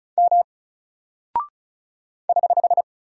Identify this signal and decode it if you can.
Morse code, heard as a keyed tone, about 36 wpm, 700 Hz. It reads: M <HH>
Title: Single Character (Letter:Number:Punctuation:Prosign) Alt 35wpm